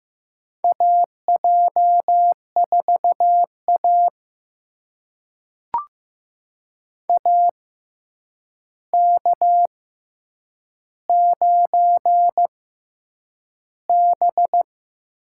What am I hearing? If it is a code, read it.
Morse code, heard as a keyed tone, about 15 wpm, spaced as for 8 wpm, 700 Hz. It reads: AJ4A AK9B
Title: USA Call Signs - 4 Characters - 6x Character Spacing - 15wpm